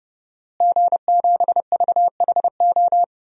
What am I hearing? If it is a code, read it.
Morse code, heard as a keyed tone, about 29 wpm, 700 Hz. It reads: G7VHO
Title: DXCC-100 Call Signs - 5 Characters - Repeated 2x - 30wpm